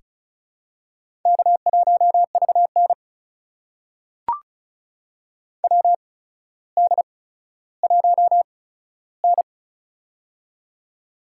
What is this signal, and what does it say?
Morse code, heard as a keyed tone, about 36 wpm, spaced as for 15 wpm, 700 Hz. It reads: K1VN WD1N
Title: USA Call Signs - 4 Characters - 8x Character Spacing - 35wpm